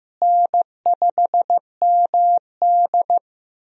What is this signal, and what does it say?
Morse code, heard as a keyed tone, about 15 wpm, 700 Hz. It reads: N5MD
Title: USA Call Signs - 4 Characters - Repeated 2x - 15wpm